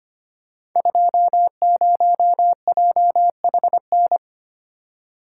Morse code, heard as a keyed tone, about 25 wpm, 700 Hz. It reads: 20JHN
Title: Code Groups - 5 Letter:Numbers - 25wpm